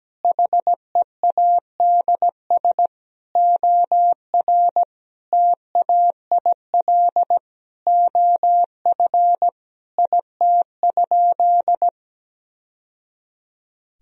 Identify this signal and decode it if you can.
Morse code, heard as a keyed tone, about 17 wpm, 700 Hz. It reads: HEADS OR TAIL OF IT?
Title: Sentences from Top 1000 Words - Review Part 3 - 17wpm